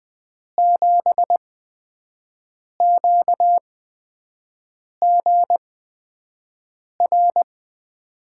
Morse code, heard as a keyed tone, about 20 wpm, spaced as for 5 wpm, 700 Hz. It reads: 7QGR
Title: USA Call Signs - 6 Characters - 8x Character Spacing - 20wpm